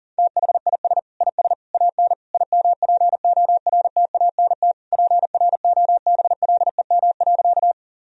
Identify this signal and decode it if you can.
Morse code, heard as a keyed tone, about 37 wpm, 700 Hz. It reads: THIS IS AN IMPORTANT PROBLEM.
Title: Sentences from Top 1000 Words - Review Part 2 - 40wpm